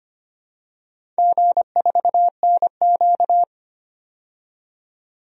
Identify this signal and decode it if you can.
Morse code, heard as a keyed tone, about 25 wpm, 700 Hz. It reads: G4NQ
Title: DXCC-100 Call Signs - 4 Characters - 25wpm